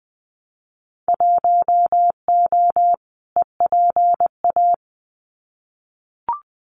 Morse code, heard as a keyed tone, about 20 wpm, 700 Hz. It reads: 1O EPA
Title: Winter Field Day Exchanges 20wpm